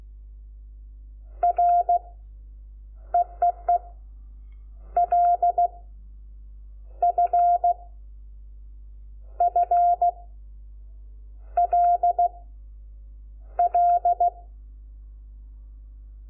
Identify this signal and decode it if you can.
Morse code, heard as a keyed tone, about 16 wpm, 700 Hz. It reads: R EEE L F F L L
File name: Lecciones / CLASE Nº5